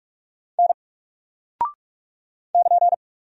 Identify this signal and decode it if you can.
Morse code, heard as a keyed tone, about 44 wpm, 700 Hz. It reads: N <KN>
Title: Single Character (Letter:Number:Punctuation:Prosign) 45wpm